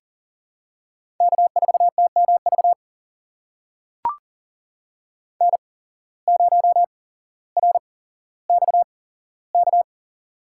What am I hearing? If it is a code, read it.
Morse code, heard as a keyed tone, about 40 wpm, spaced as for 17 wpm, 700 Hz. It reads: K4TMV N0RXK
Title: USA Call Signs - 5 Characters - 8x Character Spacing - 40wpm